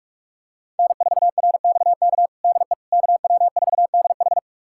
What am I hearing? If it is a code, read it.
Morse code, heard as a keyed tone, about 41 wpm, 700 Hz. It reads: N4RXK DE KW4DH
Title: USA Callsign DE Callsign - 5 Characters - 45wpm